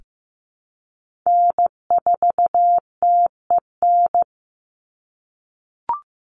Morse code, heard as a keyed tone, about 15 wpm, 700 Hz. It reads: N4TEN
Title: USA Call Signs 15wpm